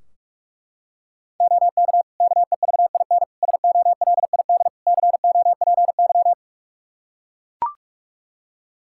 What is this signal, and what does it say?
Morse code, heard as a keyed tone, about 41 wpm, 700 Hz. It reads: OK KEVIN SOLID COPY
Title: Practice QSO - Exchange 3 - More Info - 45wpm